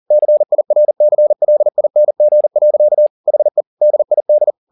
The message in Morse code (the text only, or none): CIRCLING. HE DID